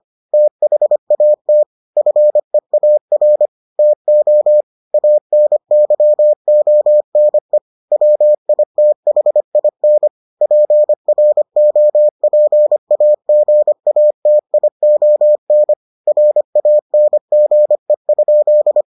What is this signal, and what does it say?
Morse code, heard as a keyed tone, about 25 wpm, 600 Hz. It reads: THAT FEAR TO ANYONE WITHIN PROPAGATION RANGE?